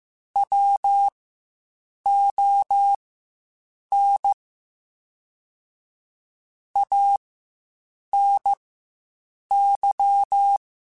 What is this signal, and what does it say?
Morse code, heard as a keyed tone, about 15 wpm, spaced as for 7 wpm, 800 Hz. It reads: WON ANY